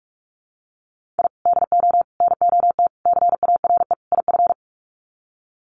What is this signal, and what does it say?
Morse code, heard as a keyed tone, about 44 wpm, 700 Hz. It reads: I DO NOT CARE IF
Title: Sets of 5 Words 45wpm